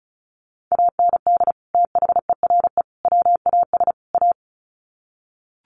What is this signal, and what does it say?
Morse code, heard as a keyed tone, about 34 wpm, 700 Hz. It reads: AND THERE WAS A
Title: Sets of 4 Words 35wpm